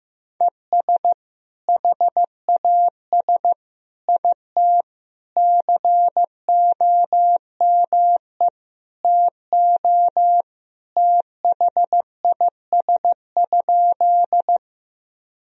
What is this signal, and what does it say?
Morse code, heard as a keyed tone, about 16 wpm, 700 Hz. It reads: ES HAS IT COME TO THIS?
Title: Sentences from Top 100 Words and Abbreviations 15wpm